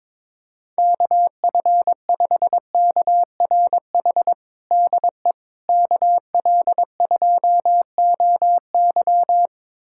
Morse code, heard as a keyed tone, about 22 wpm, 700 Hz. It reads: KF5KRH DE KL2OY
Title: USA Callsign DE Callsign 22wpm